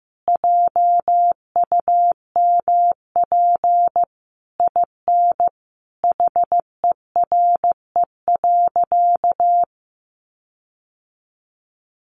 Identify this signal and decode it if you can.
Morse code, heard as a keyed tone, about 15 wpm, 700 Hz. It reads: JUMP IN HERE.
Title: Sentences from Top 1300 Words 15wpm